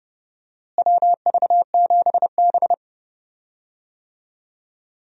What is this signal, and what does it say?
Morse code, heard as a keyed tone, about 30 wpm, 700 Hz. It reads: WV7B
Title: USA Call Signs - 4 Characters - Repeated 2x - 30wpm